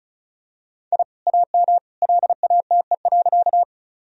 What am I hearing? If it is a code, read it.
Morse code, heard as a keyed tone, about 34 wpm, 700 Hz. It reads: I AM LATE.
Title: Sentences from Top 1000 Words - Review Part 2 - 35wpm